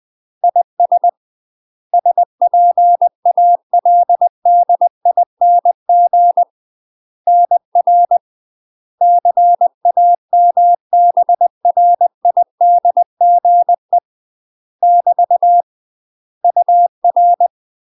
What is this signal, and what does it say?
Morse code, heard as a keyed tone, about 20 wpm, 700 Hz. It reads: IS SPALDING NR CAMBRIDGE <BT> UR
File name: UK MASTER QSOs TRAINING / 20WPM / UK MASTER QSOs 0022